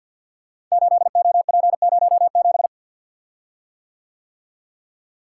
Morse code, heard as a keyed tone, about 48 wpm, 690 Hz. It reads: 8OP07
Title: Code Groups - 5 Letter:Numbers - 50wpm